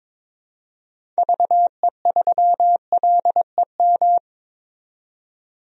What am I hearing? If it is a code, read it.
Morse code, heard as a keyed tone, about 21 wpm, 700 Hz. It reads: VE3LEM
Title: DXCC-100 Call Signs - 6 Characters - 22wpm